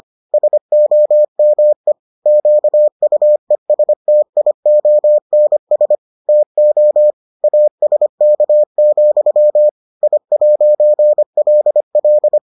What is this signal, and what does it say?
Morse code, heard as a keyed tone, about 25 wpm, 600 Hz. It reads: SOME QUESTIONS TO ASK, I'LL